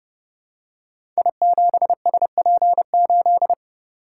Morse code, heard as a keyed tone, about 30 wpm, 700 Hz. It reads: I7SP8
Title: Code Groups - 5 Letter:Numbers - 30wpm